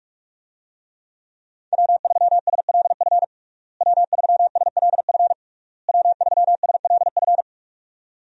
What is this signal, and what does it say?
Morse code, heard as a keyed tone, about 46 wpm, 700 Hz. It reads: W3SLF W3SLF W3SLF
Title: USA Call Signs - 5 Characters - Repeated 3x - 45wpm